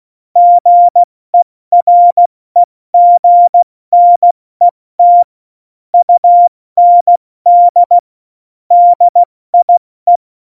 Morse code, heard as a keyed tone, about 16 wpm, 700 Hz. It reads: GEREGNET UND DIE
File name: Morsekurs / Witze / Witz 05. Januar 2021, 70 F80